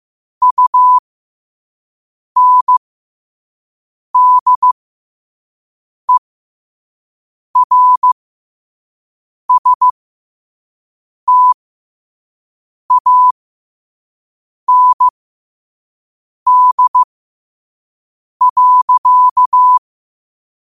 Morse code, heard as a keyed tone, about 15 wpm, spaced as for 5 wpm, 1000 Hz. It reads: UNDERSTAND.
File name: qotdfiles / 5wpm